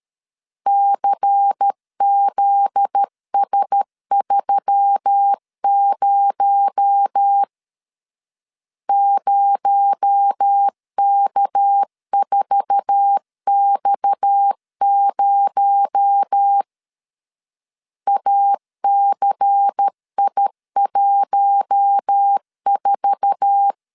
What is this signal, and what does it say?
Morse code, heard as a keyed tone, about 13 wpm, 800 Hz. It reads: CZS30 0K4X0 ACI14
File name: CW / DET / 1 9 1500C S A 12wpm 3800